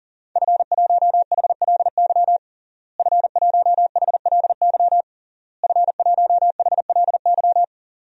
Morse code, heard as a keyed tone, about 40 wpm, 700 Hz. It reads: F1HLY F1HLY F1HLY
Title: DXCC-100 Call Signs - 5 Characters - Repeated 3x - 40wpm